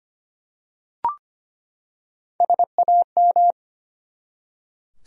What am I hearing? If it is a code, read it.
Morse code, heard as a keyed tone, about 25 wpm, 700 Hz. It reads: SAM
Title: USA Common Names - Max 4 Letters - 25wpm